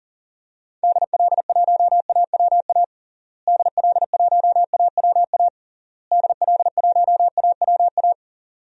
Morse code, heard as a keyed tone, about 40 wpm, 700 Hz. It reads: DL1AWA DL1AWA DL1AWA
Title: DXCC-100 Call Signs - 6 Characters - Repeated 3x - 40wpm